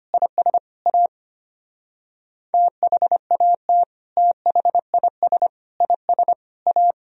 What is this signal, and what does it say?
Morse code, heard as a keyed tone, about 32 wpm, 700 Hz. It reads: IS A THAT THIS IS A